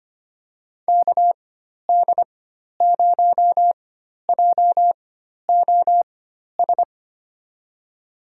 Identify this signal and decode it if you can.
Morse code, heard as a keyed tone, about 25 wpm, spaced as for 12 wpm, 700 Hz. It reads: KD0JOS
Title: USA Call Signs - 6 Characters - 4x Character Spacing - 25wpm